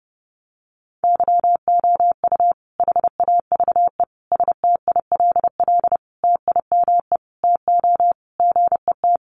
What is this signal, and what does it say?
Morse code, heard as a keyed tone, about 30 wpm, 700 Hz. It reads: YOU HAVE STILL TIME TO GET
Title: Sentences from Top 600 Words 30wpm